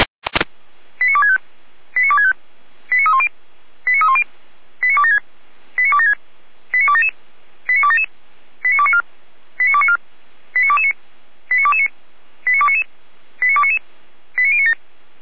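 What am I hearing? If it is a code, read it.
Five-tone selective call (ZVEI), heard as quick runs of five tones, tones of about 70 ms. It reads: 89265 89265 89310 89310 89276 89276 89277 89277 89294 89294 89229 89229 89299 89299 89097
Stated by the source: Fire brigade ZVEI five-tone code 89160 received on an SDR, feuerwehr-orlamuende.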